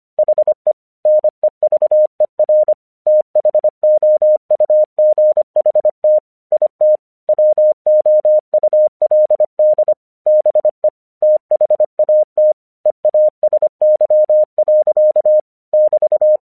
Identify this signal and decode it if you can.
Morse code, heard as a keyed tone, about 25 wpm, 610 Hz. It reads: HE NEVER THOUGHT IT WOULD BE THAT EASY. <BT>
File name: 1 English 25wpm / AQC32